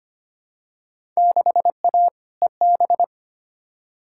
Morse code, heard as a keyed tone, about 25 wpm, 700 Hz. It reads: 6A EB
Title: Field Day Exchanges Encore - 25wpm